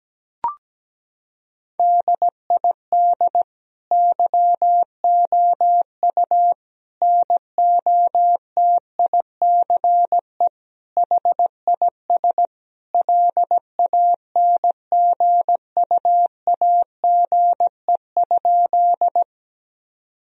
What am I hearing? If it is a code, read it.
Morse code, heard as a keyed tone, about 17 wpm, 700 Hz. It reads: DID YOU NOTICE HIS LANGUAGE?
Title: Sentences from Top 1000 Words - Review Part 3 - 17wpm